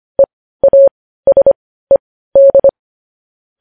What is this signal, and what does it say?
Morse code, heard as a keyed tone, about 25 wpm, spaced as for 15 wpm, 550 Hz. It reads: EASED
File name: podcast / News headlines-1x-25x15